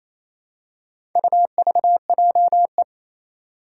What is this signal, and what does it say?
Morse code, heard as a keyed tone, about 28 wpm, 700 Hz. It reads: UVJE
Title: Four Challenging Letter-Number Combinations 28wpm